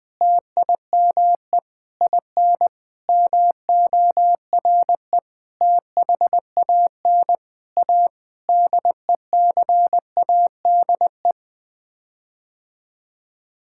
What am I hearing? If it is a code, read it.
Morse code, heard as a keyed tone, about 20 wpm, 700 Hz. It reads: TIME IN MORE THAN A DECADE
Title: Sets of 9 Words - 20wpm